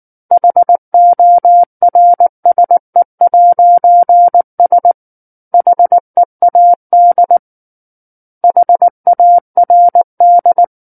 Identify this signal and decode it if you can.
Morse code, heard as a keyed tone, about 19 wpm, 700 Hz. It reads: HORSE'S HEAD HARD
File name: Book One / 10-Chapter 10 - 19 WPM